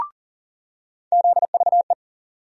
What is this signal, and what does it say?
Morse code, heard as a keyed tone, about 40 wpm, 700 Hz. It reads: ZVE
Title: USA Call Sign Postfixes - 3 Letters 40wpm